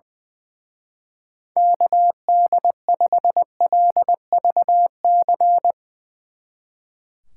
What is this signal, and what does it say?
Morse code, heard as a keyed tone, about 22 wpm, 700 Hz. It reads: KD5LVC
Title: Speed Racing - USA 2x3 Call Signs 20wpm